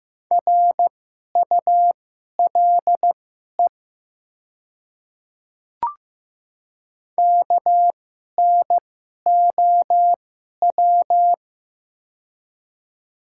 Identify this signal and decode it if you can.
Morse code, heard as a keyed tone, about 15 wpm, spaced as for 11 wpm, 700 Hz. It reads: RULE KNOW
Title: 2x Character Spacing - Top 500 Words - 15wpm